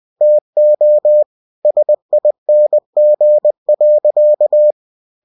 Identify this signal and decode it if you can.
Morse code, heard as a keyed tone, about 20 wpm, 600 Hz. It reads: TO SING.
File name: cw-books / winnie-the-pooh / winnie-20wpm-page-0081